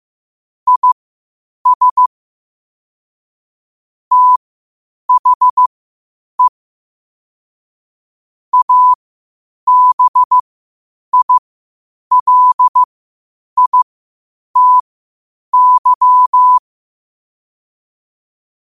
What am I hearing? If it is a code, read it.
Morse code, heard as a keyed tone, about 15 wpm, spaced as for 8 wpm, 1000 Hz. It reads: IS THE ABILITY
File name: qotdfiles / 8wpm